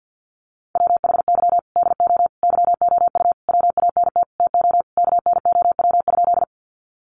Morse code, heard as a keyed tone, about 50 wpm, 700 Hz. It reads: WHY DO YOU WANT TO KNOW?
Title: Sentences from Top 200 Words 50wpm